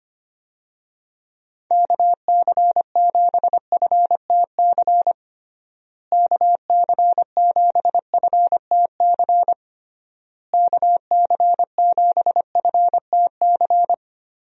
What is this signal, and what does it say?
Morse code, heard as a keyed tone, about 25 wpm, 700 Hz. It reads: KC7FTC KC7FTC KC7FTC
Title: USA Call Signs - 6 Characters - Repeated 3x - 25wpm